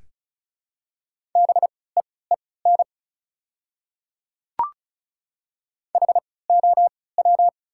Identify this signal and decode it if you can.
Morse code, heard as a keyed tone, about 34 wpm, spaced as for 20 wpm, 700 Hz. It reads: BEEN HOW
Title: 3x Character Spacing - Top 100 Words - 35wpm